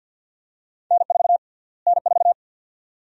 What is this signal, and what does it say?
Morse code, heard as a keyed tone, about 50 wpm, 700 Hz. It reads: N4 N4
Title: USA Call Sign Prefixes - Repeated 2x - 50wpm